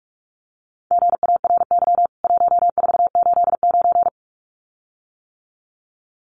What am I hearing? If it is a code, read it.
Morse code, heard as a keyed tone, about 44 wpm, 700 Hz. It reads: GARY 1489
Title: CWT Exchange 45wpm